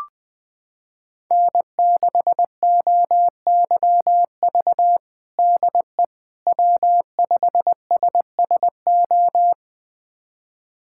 Morse code, heard as a keyed tone, about 20 wpm, 700 Hz. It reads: N6OYV DE W5SSO
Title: USA Callsign DE Callsign - 5 Characters - 20wpm